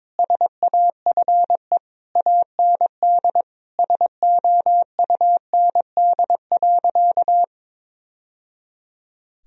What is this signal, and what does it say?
Morse code, heard as a keyed tone, about 22 wpm, 700 Hz. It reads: SAFE AND SOUND.
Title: Sentences from Top 1000 Words - Review Part 1 - 22wpm